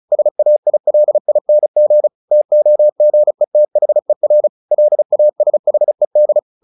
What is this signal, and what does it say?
Morse code, heard as a keyed tone, about 34 wpm, 600 Hz. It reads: SAILING TOGETHER LASHED
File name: the old man and the sea 2 35wpm